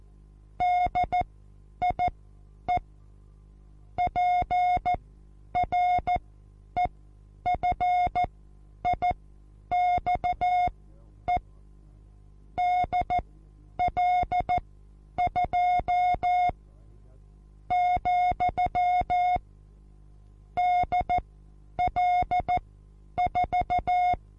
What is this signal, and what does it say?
Morse code, heard as a keyed tone, about 14 wpm, spaced as for 9 wpm, 720 Hz. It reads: DIE PREFIXE DL2, DL4
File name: Onlinelehrgang / morsen / DJ4UF Morsen23